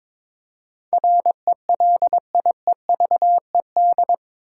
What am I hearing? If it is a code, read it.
Morse code, heard as a keyed tone, about 22 wpm, 700 Hz. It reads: RELIEVED
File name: practice / 22wpm / Eight Letter Words - 22wpm